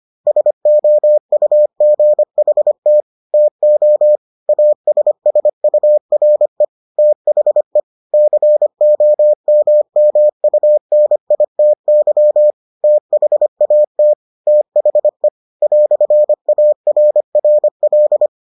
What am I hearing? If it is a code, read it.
Morse code, heard as a keyed tone, about 25 wpm, 600 Hz. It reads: SOUGHT TO ASSURE THE COMMUNITY THAT THE "ARRL